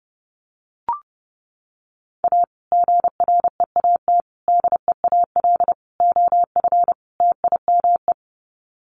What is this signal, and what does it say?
Morse code, heard as a keyed tone, about 30 wpm, 700 Hz. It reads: A GREAT DEAL OF TIME
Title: Sets of 5 Words Encore2 30wpm